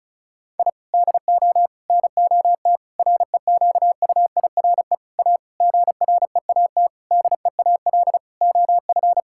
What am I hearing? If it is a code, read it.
Morse code, heard as a keyed tone, about 36 wpm, 700 Hz. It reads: I DO NOT REQUIRE A GREAT DEAL OF
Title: Sentences from Top 1000 Words - Review Part 6 - 35wpm